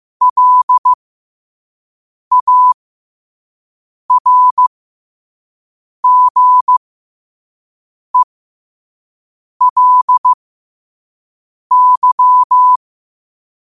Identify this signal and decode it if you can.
Morse code, heard as a keyed tone, about 15 wpm, spaced as for 5 wpm, 1000 Hz. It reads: LARGELY
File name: 5wpm